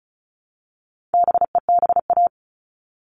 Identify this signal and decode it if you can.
Morse code, heard as a keyed tone, about 34 wpm, 700 Hz. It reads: BEBA
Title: All Cut Numbers - 4-Digit 35wpm